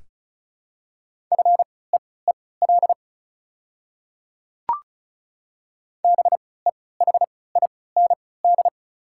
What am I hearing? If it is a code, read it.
Morse code, heard as a keyed tone, about 34 wpm, spaced as for 20 wpm, 700 Hz. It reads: FEEL BEHIND
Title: 3x Character Spacing - Top 500 Words - 35wpm